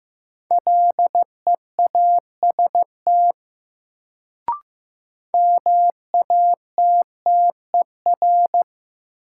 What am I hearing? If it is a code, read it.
Morse code, heard as a keyed tone, about 15 wpm, 700 Hz. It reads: LEAST MATTER
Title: Sentences from Top 700 Words - New Word Drill - 15wpm